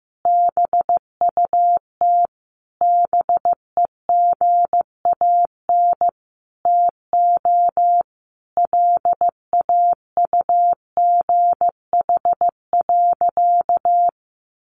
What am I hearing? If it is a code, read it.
Morse code, heard as a keyed tone, about 15 wpm, 700 Hz. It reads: BUT BEGAN TO LAUGH.
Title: Sentences from Top 600 Words 15wpm